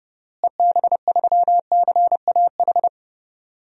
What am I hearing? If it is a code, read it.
Morse code, heard as a keyed tone, about 30 wpm, 700 Hz. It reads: EB3CAH
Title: DXCC-100 Call Signs - 6 Characters - Repeated 2x - 30wpm